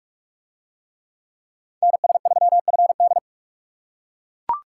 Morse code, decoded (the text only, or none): NS3FD